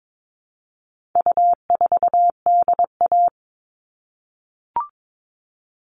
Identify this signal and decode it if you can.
Morse code, heard as a keyed tone, about 22 wpm, 700 Hz. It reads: U4DA
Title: All Cut Numbers - 4-Digit 22wpm